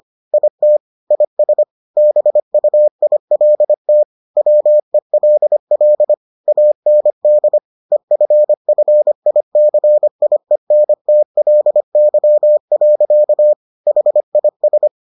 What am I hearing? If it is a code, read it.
Morse code, heard as a keyed tone, about 25 wpm, 600 Hz. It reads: IT IS BUILT WELL AND EFFICIENTLY. HIS